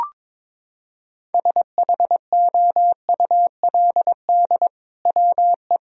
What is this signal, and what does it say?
Morse code, heard as a keyed tone, about 22 wpm, 700 Hz. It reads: SHOULD WE
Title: Sentences from Top 1000 Words - Review Part 3 - 22wpm